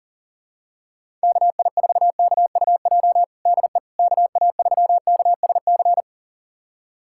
Morse code, heard as a keyed tone, about 40 wpm, 700 Hz. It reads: KI4KUJ DE KA3KSC
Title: USA Callsign DE Callsign 40wpm